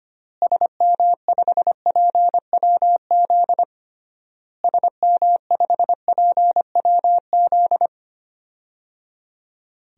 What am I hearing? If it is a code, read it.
Morse code, heard as a keyed tone, about 25 wpm, 700 Hz. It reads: SM5PWZ SM5PWZ
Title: DXCC-100 Call Signs - 6 Characters - Repeated 3x - 25wpm